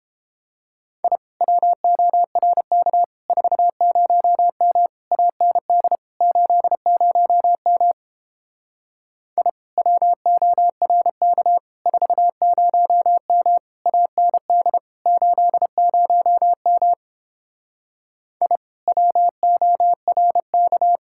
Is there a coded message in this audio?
Yes: Morse code from 1.0 to 21.0 s, heard as a keyed tone, about 31 wpm, 700 Hz. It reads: I WORK 40M AND 80M I WORK 40M AND 80M I WORK